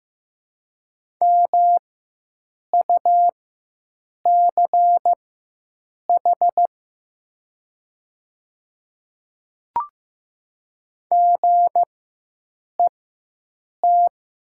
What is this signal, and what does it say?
Morse code, heard as a keyed tone, about 15 wpm, spaced as for 7 wpm, 700 Hz. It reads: MUCH GET
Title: 4x Character Spacing - Top 200 Words - 15wpm